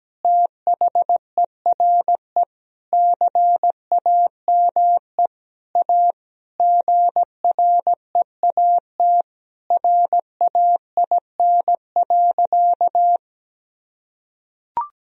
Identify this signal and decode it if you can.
Morse code, heard as a keyed tone, about 17 wpm, 700 Hz. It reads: THERE CAME A GREAT RAIN.
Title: Sentences from Top 1000 Words - Review Part 3 - 17wpm